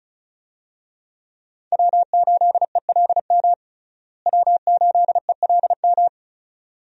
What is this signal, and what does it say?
Morse code, heard as a keyed tone, about 36 wpm, 700 Hz. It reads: W8ELM W8ELM
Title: USA Call Signs - 5 Characters - Repeated 2x - 35wpm